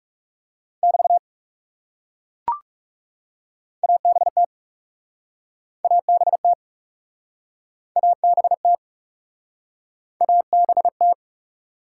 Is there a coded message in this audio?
Yes: Morse code from 0.8 to 11.1 s, heard as a keyed tone, about 42 wpm, 700 Hz. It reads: <BT> ABT ABT ABT ABT